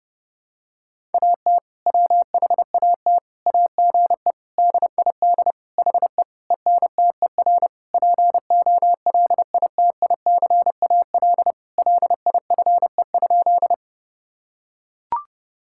Morse code, heard as a keyed tone, about 30 wpm, 700 Hz. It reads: AT WHAT AGE DID HE ENTER POLITICAL LIFE?
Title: Sentences from Top 1000 Words - Review Part 5 - 30wpm